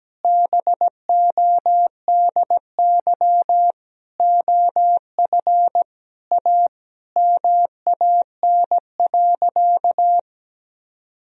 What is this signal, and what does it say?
Morse code, heard as a keyed tone, about 17 wpm, 700 Hz. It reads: BODY OF A MAN.
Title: Sentences from Top 1000 Words - Review Part 2 - 17wpm